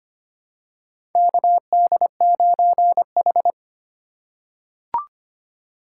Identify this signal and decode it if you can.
Morse code, heard as a keyed tone, about 25 wpm, 690 Hz. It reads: KD9H
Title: USA Call Signs - 4x Character Spacing - 25wpm